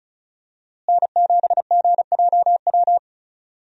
Morse code, heard as a keyed tone, about 36 wpm, 700 Hz. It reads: N7GJW
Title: USA Call Signs - 5 Characters - Repeated 2x - 35wpm